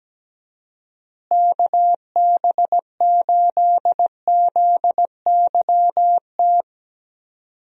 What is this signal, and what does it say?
Morse code, heard as a keyed tone, about 17 wpm, 700 Hz. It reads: KB8ZYT
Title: USA Call Signs - 6 Characters - Repeated 3x - 17wpm